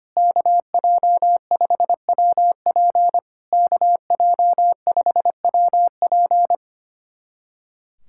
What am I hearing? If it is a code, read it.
Morse code, heard as a keyed tone, about 25 wpm, 700 Hz. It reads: KJ5WP KJ5WP
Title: SOTA QSO Practice 25wpm